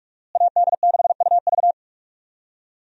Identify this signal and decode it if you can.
Morse code, heard as a keyed tone, about 46 wpm, 700 Hz. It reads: AD6UV
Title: USA Call Signs - 5 Characters - Repeated 3x - 45wpm